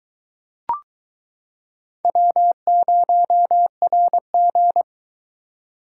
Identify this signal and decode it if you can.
Morse code, heard as a keyed tone, about 22 wpm, 700 Hz. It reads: W0RG